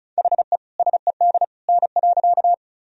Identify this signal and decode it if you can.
Morse code, heard as a keyed tone, about 36 wpm, 700 Hz. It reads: HE SED N.
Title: Sentences from Top 100 Words and Abbreviations 35wpm